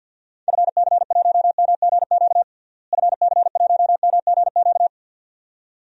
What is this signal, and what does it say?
Morse code, heard as a keyed tone, about 46 wpm, 700 Hz. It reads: FC1MGQ FC1MGQ
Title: DXCC-100 Call Signs - 6 Characters - Repeated 2x - 50wpm